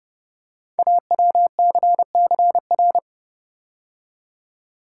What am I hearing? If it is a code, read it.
Morse code, heard as a keyed tone, about 30 wpm, 700 Hz. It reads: AWCCR
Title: Code Groups - 5 Letters - 30wpm